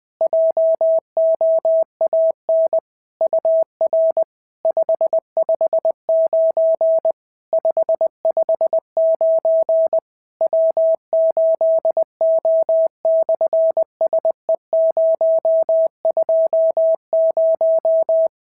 Morse code, heard as a keyed tone, about 20 wpm, 650 Hz. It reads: JOAN UR 559 559 W8O/SE020
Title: SOTA QSO Practice 20wpm